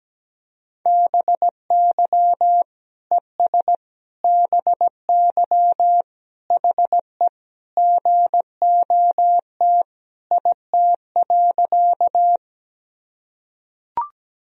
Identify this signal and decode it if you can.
Morse code, heard as a keyed tone, about 17 wpm, 700 Hz. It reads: BY ES BY HE GOT IT.
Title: Sentences from Top 300 Words and Abbreviations 17wpm